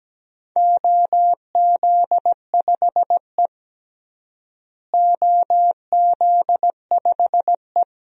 Morse code, heard as a keyed tone, about 17 wpm, 700 Hz. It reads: OZ5E OZ5E
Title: DXCC-100 Call Signs - 4 Characters - Repeated 3x - 17wpm